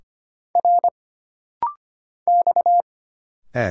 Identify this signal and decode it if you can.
Morse code, heard as a keyed tone, about 25 wpm, 700 Hz. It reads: R X E
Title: Single Letters - Rapid-Fire 25wpm